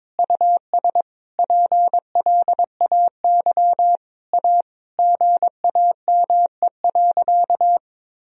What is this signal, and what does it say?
Morse code, heard as a keyed tone, about 22 wpm, 700 Hz. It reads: US PLAY A GAME.
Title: Sentences from Top 1000 Words - Review Part 4 - 22wpm